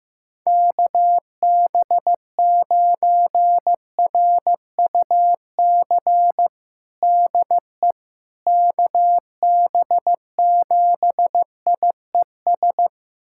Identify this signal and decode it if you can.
Morse code, heard as a keyed tone, about 15 wpm, 700 Hz. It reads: KB9RUC DE KB7IES
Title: USA Callsign DE Callsign - 6 Characters - 15wpm